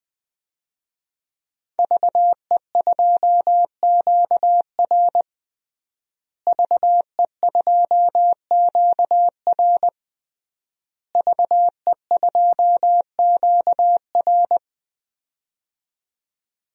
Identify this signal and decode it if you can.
Morse code, heard as a keyed tone, about 21 wpm, 700 Hz. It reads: VE2QR VE2QR VE2QR
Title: DXCC-100 Call Signs - 5 Characters - Repeated 3x - 20wpm